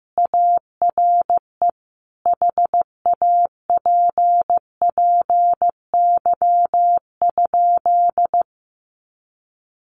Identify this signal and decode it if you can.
Morse code, heard as a keyed tone, about 15 wpm, 700 Hz. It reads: ARE HAPPY?
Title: Sentences from Top 1000 Words 15wpm